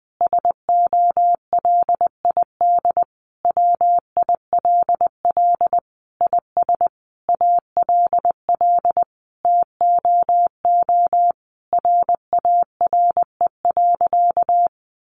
Morse code, heard as a keyed tone, about 20 wpm, 700 Hz. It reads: SOLID WILL IS ALL TOO RARE.
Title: Sentences from Top 1500 Words 20wpm